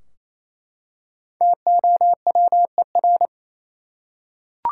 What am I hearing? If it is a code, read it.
Morse code, heard as a keyed tone, about 28 wpm, 700 Hz. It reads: TOWER
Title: Five Letter Words - 28wpm